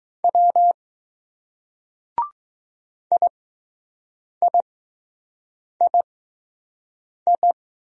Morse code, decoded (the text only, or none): W IIII